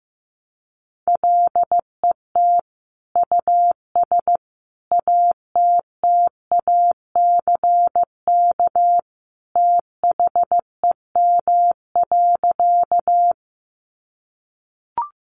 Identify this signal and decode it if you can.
Morse code, heard as a keyed tone, about 15 wpm, 700 Hz. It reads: LET US ATTACK THEM.
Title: Sentences from Top 1500 Words 15wpm